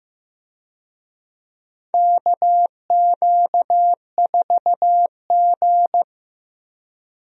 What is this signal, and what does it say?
Morse code, heard as a keyed tone, about 15 wpm, 700 Hz. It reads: KQ4G
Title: USA Call Signs - 4 Characters - Repeated 2x - 15wpm